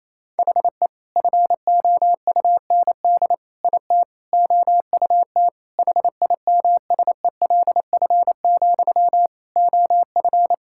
Morse code, heard as a keyed tone, about 28 wpm, 700 Hz. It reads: HE FOUND IT OUT HIMSELF, OF
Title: Sentences from Top 1600 Words 28wpm